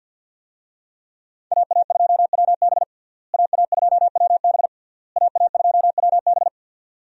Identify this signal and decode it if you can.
Morse code, heard as a keyed tone, about 46 wpm, 700 Hz. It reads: AA2WB AA2WB AA2WB
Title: USA Call Signs - 5 Characters - Repeated 3x - 50wpm